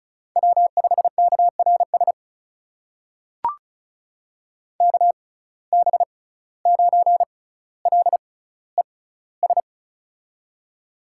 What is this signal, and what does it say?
Morse code, heard as a keyed tone, about 36 wpm, spaced as for 17 wpm, 700 Hz. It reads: W5KRS KB9LES
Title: USA Call Signs - 6x Character Spacing - 35wpm